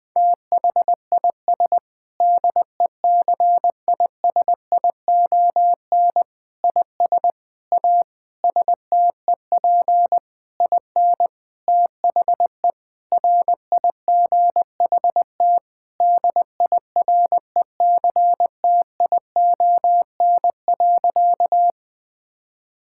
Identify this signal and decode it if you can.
Morse code, heard as a keyed tone, about 20 wpm, 700 Hz. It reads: THIS DECISION IS A STEP IN THE RIGHT DIRECTION.
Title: Sentences with 7-Word Phrases - 20wpm